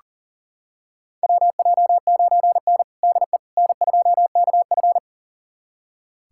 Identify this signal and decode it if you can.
Morse code, heard as a keyed tone, about 40 wpm, 700 Hz. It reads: WJ9N DE N2KF
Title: USA Callsign DE Callsign - 4 Characters - 40wpm